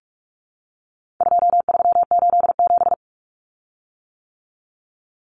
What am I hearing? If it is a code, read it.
Morse code, heard as a keyed tone, about 44 wpm, 700 Hz. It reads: 2387
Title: Common Cut Numbers - 4-Digit 45wpm